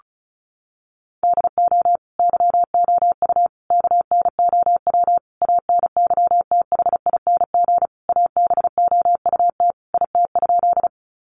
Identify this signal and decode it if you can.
Morse code, heard as a keyed tone, about 34 wpm, 700 Hz. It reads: DO YOU KNOW ANYTHING ABOUT IT?
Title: Sentences from Top 1000 Words 35wpm